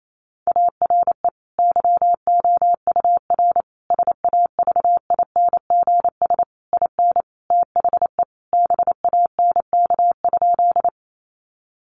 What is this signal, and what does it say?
Morse code, heard as a keyed tone, about 28 wpm, 700 Hz. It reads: ARE YOUR SAVINGS IN THE BANK?
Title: Sentences from Top 1200 Words 28wpm